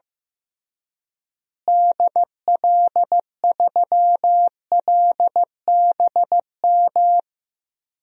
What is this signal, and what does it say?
Morse code, heard as a keyed tone, about 15 wpm, 700 Hz. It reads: DL3LBM
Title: DXCC-100 Call Signs - 6 Characters - Repeated 3x - 15wpm